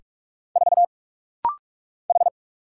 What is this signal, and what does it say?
Morse code, heard as a keyed tone, about 45 wpm, 700 Hz. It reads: 4 H
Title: Single Letter-Number - Rapid-Fire 45wpm